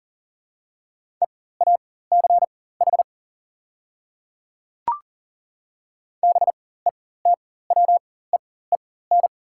Morse code, heard as a keyed tone, about 40 wpm, spaced as for 19 wpm, 700 Hz. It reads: EACH BETWEEN
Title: 4x Character Spacing - Top 300 Words - 40wpm